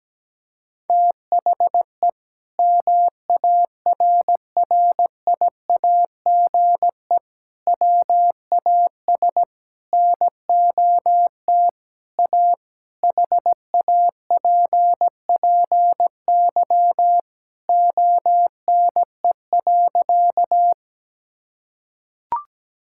Morse code, heard as a keyed tone, about 17 wpm, 700 Hz. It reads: THE MARRIAGE WAS NOT A HAPPY ONE.
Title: Sentences from Top 1000 Words - Review Part 5 - 17wpm